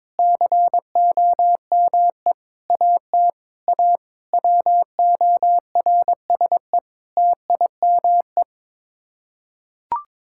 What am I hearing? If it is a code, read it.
Morse code, heard as a keyed tone, about 22 wpm, 700 Hz. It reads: COME AT A WORSE TIME
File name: practice / 22wpm / Sets of 8 Words - 22wpm